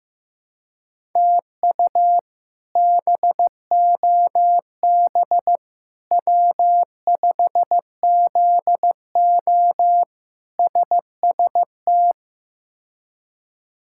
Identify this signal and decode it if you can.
Morse code, heard as a keyed tone, about 15 wpm, 700 Hz. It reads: TU BOB W5ZO SST
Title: K1USN SST QSO Practice - 15wpm